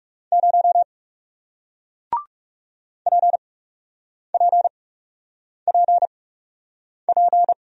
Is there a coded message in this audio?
Yes: Morse code from 0.3 to 7.5 s, heard as a keyed tone, about 40 wpm, spaced as for 8 wpm, 700 Hz. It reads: 0 PPPP